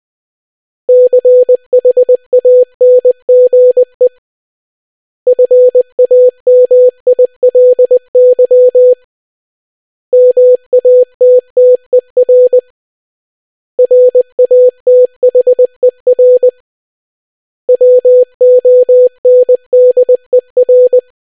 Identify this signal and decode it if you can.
Morse code, heard as a keyed tone, about 20 wpm, 500 Hz. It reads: CHANGE FAMILY MATTER RATHER WONDER